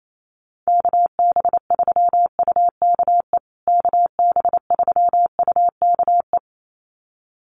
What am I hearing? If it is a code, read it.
Morse code, heard as a keyed tone, about 28 wpm, 700 Hz. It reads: KB3UKE KB3UKE
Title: USA Call Signs Repeated 2x 28wpm